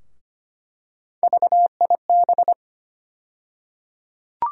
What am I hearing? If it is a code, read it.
Morse code, heard as a keyed tone, about 25 wpm, 700 Hz. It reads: VIB
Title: USA Call Sign Postfixes - 3 Letters 25wpm